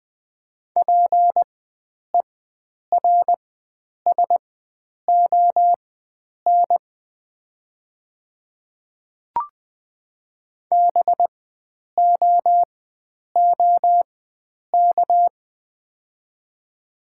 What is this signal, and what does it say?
Morse code, heard as a keyed tone, about 20 wpm, spaced as for 9 wpm, 700 Hz. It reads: PERSON BOOK
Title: 4x Character Spacing - Top 500 Words - 20wpm